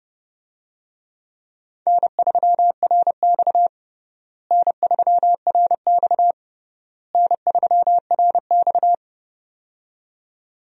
Morse code, heard as a keyed tone, about 32 wpm, 700 Hz. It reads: N3RX N3RX N3RX
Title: USA Call Signs - 4 Characters - Repeated 3x - 30wpm